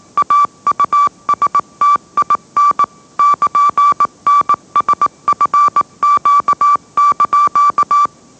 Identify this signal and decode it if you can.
Morse code, heard as a keyed tone, about 18 wpm, 1160 Hz. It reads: AUSTIN <KN>NSFQ)
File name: mrx / mrx082 20wpm